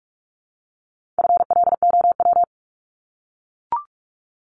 Morse code, heard as a keyed tone, about 44 wpm, 700 Hz. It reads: FLOW
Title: Top 1001-1250 Words 45wpm